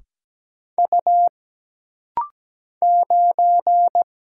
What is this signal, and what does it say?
Morse code, heard as a keyed tone, about 17 wpm, 700 Hz. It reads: U 9
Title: Single Letter-Number - Rapid-Fire 17wpm